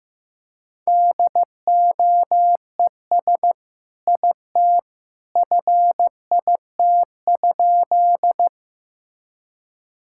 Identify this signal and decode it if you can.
Morse code, heard as a keyed tone, about 15 wpm, 700 Hz. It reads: DOES IT FIT?
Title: Sentences from Top 1000 Words - Review Part 5 - 15wpm